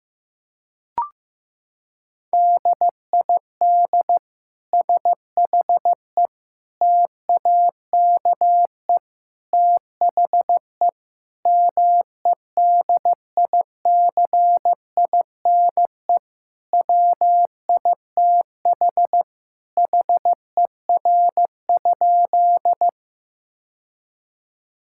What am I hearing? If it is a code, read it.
Morse code, heard as a keyed tone, about 15 wpm, 700 Hz. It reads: DID SHE TAKE THE MEDICINE WITH HER?
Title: Sentences from Top 1000 Words - Review Part 6 - 15wpm